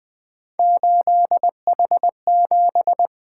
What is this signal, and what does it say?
Morse code, heard as a keyed tone, about 20 wpm, 700 Hz. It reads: 8H7
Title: Three Letter-Number Combinations - 20wpm